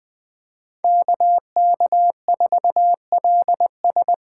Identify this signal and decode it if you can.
Morse code, heard as a keyed tone, about 21 wpm, 700 Hz. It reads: KK4LS
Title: USA Call Signs - 5 Characters - 2x Character Spacing - 20wpm